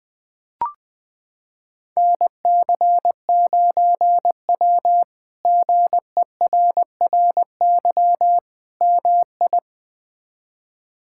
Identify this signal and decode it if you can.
Morse code, heard as a keyed tone, about 20 wpm, 700 Hz. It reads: NC9W GERRY MI
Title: K1USN SST QSO Practice - 20wpm